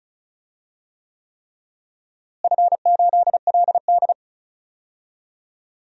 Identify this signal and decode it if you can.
Morse code, heard as a keyed tone, about 36 wpm, 700 Hz. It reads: F8LD
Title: DXCC-100 Call Signs - 4 Characters - Repeated 3x - 35wpm